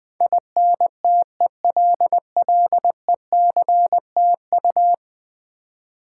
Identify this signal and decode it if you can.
Morse code, heard as a keyed tone, about 21 wpm, 700 Hz. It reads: INTELLECTU
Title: Build A Word - 5 Letters At A Time - 12-Letter Words - 20wpm